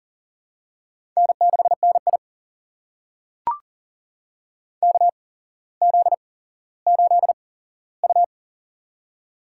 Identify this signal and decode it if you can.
Morse code, heard as a keyed tone, about 40 wpm, spaced as for 17 wpm, 700 Hz. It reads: N6NI KZ8U
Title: USA Call Signs - 4 Characters - 8x Character Spacing - 40wpm